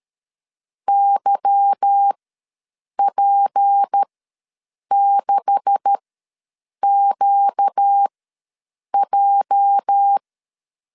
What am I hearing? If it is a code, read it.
Morse code, heard as a keyed tone, about 13 wpm, spaced as for 7 wpm, 800 Hz. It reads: YP6QJ
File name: LEZIONI / 9 286 ws 12wpm 3351